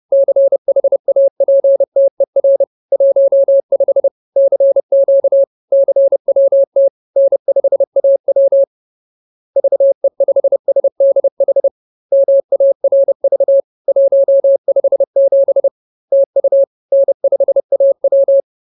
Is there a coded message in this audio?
Yes: Morse code from 0.1 to 18.4 s, heard as a keyed tone, about 30 wpm, 550 Hz. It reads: CHAPTER 15 CQ CWT N5AW VE5SDH MARV 157 TU N5AW